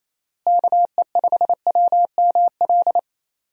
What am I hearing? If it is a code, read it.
Morse code, heard as a keyed tone, about 28 wpm, 700 Hz. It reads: KE5WML
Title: USA Call Signs - 6 Characters - Repeated 2x - 28wpm